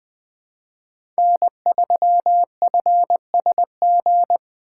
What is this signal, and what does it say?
Morse code, heard as a keyed tone, about 21 wpm, 700 Hz. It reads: N3FSG
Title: USA Call Signs - 2x Character Spacing - 20wpm